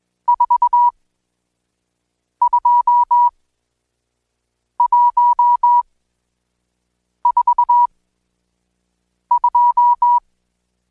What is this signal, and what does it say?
Morse code, heard as a keyed tone, about 20 wpm, spaced as for 5 wpm, 970 Hz. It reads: 42142
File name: Code Quick / AudioDownld1 / Seg76-Day8-DayDrill 8